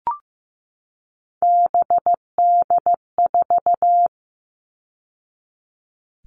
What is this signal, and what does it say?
Morse code, heard as a keyed tone, about 15 wpm, 700 Hz. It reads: BD4
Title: All Cut Numbers - 3-Digit 15wpm